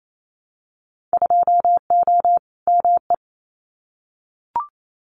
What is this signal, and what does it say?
Morse code, heard as a keyed tone, about 28 wpm, 700 Hz. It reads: 2O ME
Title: Winter Field Day Exchanges 28wpm